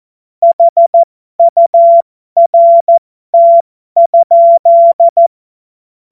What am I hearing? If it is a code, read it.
Morse code, heard as a keyed tone, about 14 wpm, spaced as for 12 wpm, 680 Hz. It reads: HURT?
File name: Morsekurs / Witze